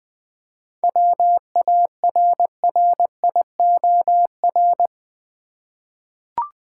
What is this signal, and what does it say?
Morse code, heard as a keyed tone, about 20 wpm, 700 Hz. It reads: WARRIOR
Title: 7 Letter Words - First 3 Letters Given - 20wpm